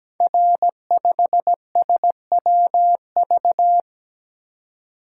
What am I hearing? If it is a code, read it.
Morse code, heard as a keyed tone, about 17 wpm, 700 Hz. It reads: R5SWV
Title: Code Groups - 5 Letter:Numbers - 17wpm